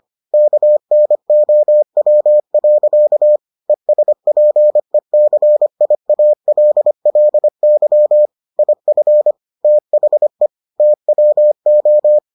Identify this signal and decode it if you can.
Morse code, heard as a keyed tone, about 25 wpm, 600 Hz. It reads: KNOW. ESPECIALLY IF THE TWO